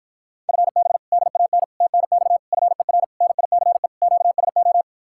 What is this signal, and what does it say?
Morse code, heard as a keyed tone, about 53 wpm, 700 Hz. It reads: FB DAN TNX FER NICE QSO